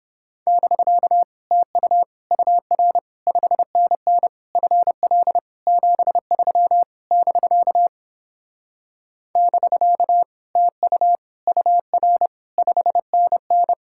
Morse code, heard as a keyed tone, about 31 wpm, 700 Hz. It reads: <BK> TU UR 5NN FL 73 <BK> <BK> TU UR 5NN